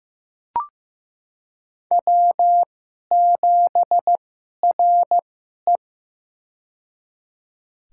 Morse code, heard as a keyed tone, about 16 wpm, spaced as for 11 wpm, 700 Hz. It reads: W7RE
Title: USA Call Signs - 4 Characters - 2x Character Spacing - 15wpm